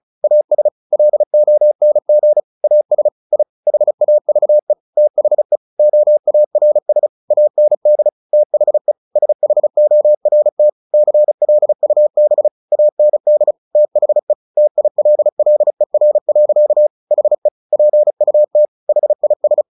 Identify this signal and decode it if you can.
Morse code, heard as a keyed tone, about 34 wpm, 600 Hz. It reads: AS LONG AS I HAVE THE OARS AND THE SHORT CLUB AND THE TILLER. HE PUT HIS
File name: the old man and the sea 2 35wpm